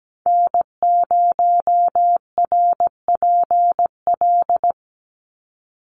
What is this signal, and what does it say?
Morse code, heard as a keyed tone, about 17 wpm, 700 Hz. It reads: N0RPL
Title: USA Call Signs 17wpm